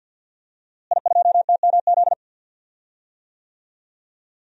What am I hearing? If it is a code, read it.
Morse code, heard as a keyed tone, about 52 wpm, 700 Hz. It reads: I2TMZ
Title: DXCC-100 Call Signs - 5 Characters - Repeated 3x - 50wpm